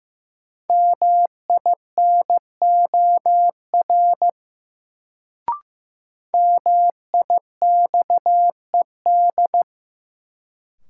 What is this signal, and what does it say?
Morse code, heard as a keyed tone, about 15 wpm, 700 Hz. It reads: MINOR MIXED
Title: Sentences from Top 1500 Words - New Word Drill Alt - 15wpm